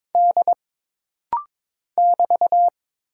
Morse code, heard as a keyed tone, about 22 wpm, 700 Hz. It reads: D <BT>
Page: This rapid-fire timing is geared to push your ICR (Instant Character Recognition) skills.